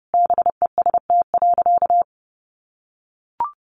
Morse code, heard as a keyed tone, about 30 wpm, 700 Hz. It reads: BEST.
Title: Sentences from Top 500 Words 30wpm